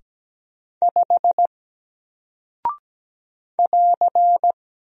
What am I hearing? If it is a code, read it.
Morse code, heard as a keyed tone, about 17 wpm, 700 Hz. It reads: 5 <AR>
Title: Single Character (Letter:Number:Punctuation:Prosign) Alt 17wpm